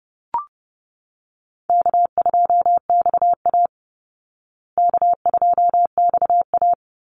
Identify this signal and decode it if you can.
Morse code, heard as a keyed tone, about 30 wpm, 700 Hz. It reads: K2XA K2XA
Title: Easy Call Signs Repeated 3x - 30wpm